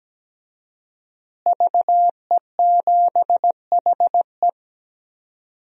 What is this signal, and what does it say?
Morse code, heard as a keyed tone, about 17 wpm, 700 Hz. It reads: VE7HE
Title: DXCC-100 Call Signs - 5 Characters - Repeated 2x - 17wpm